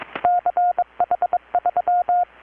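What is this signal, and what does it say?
Morse code, heard as a keyed tone, about 20 wpm, 720 Hz. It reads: CH3
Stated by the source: Morse from KPH's Night of Nights XX maritime radiotelegraphy event, received in Chiapas, Mexico about 3,800 km away on a Kenwood R-600 with a grounded 10 m long-wire antenna.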